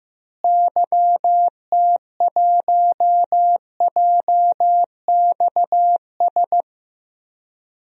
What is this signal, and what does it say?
Morse code, heard as a keyed tone, about 15 wpm, 700 Hz. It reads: YT1JXS
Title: DXCC-100 Call Signs - 6 Characters - Repeated 2x - 15wpm